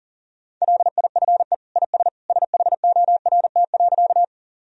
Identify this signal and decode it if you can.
Morse code, heard as a keyed tone, about 40 wpm, 700 Hz. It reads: LIFE IS SHORT.
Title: Sentences from Top 1000 Words - Review Part 2 - 40wpm